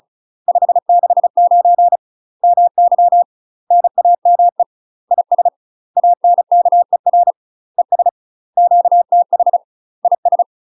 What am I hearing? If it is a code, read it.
Morse code, heard as a keyed tone, about 34 wpm, 700 Hz. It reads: 569 MY NAME IS ANKER ES QTH IS
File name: UK MASTER QSOs TRAINING / 35WPM / UK MASTER QSOs 0040